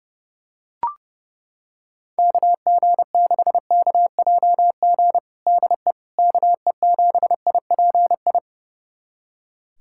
Morse code, heard as a keyed tone, about 30 wpm, 700 Hz. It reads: KG6KJG DE KE7IPI
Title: USA Callsign DE Callsign - 6 Characters - 30wpm